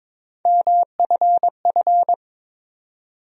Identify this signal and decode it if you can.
Morse code, heard as a keyed tone, about 23 wpm, 700 Hz. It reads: MFF
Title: USA Call Sign Postfixes - 3 Letters 22wpm